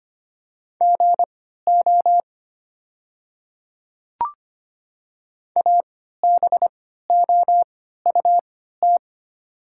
Morse code, heard as a keyed tone, about 25 wpm, spaced as for 14 wpm, 690 Hz. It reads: GO ABOUT
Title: 3x Character Spacing - Top 100 Words - 25wpm